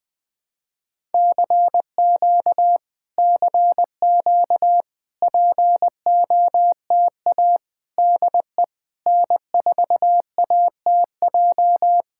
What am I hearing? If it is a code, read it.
Morse code, heard as a keyed tone, about 20 wpm, 700 Hz. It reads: CQ CQ POTA DE N4ATJ
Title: POTA QSO Practice - 20wpm